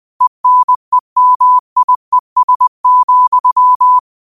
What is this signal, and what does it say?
Morse code, heard as a keyed tone, about 20 wpm, 1000 Hz. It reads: ENEMIES,